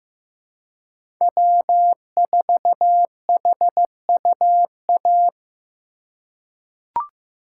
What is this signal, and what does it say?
Morse code, heard as a keyed tone, about 16 wpm, 700 Hz. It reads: W4HUA
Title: USA Call Signs - 3x Character Spacing - 15wpm